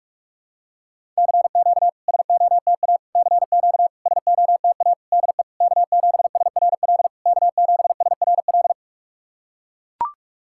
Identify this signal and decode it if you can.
Morse code, heard as a keyed tone, about 44 wpm, 690 Hz. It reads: CQ SOTA CQ SOTA DE K7SRL K7SRL
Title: SOTA QSO Practice 45wpm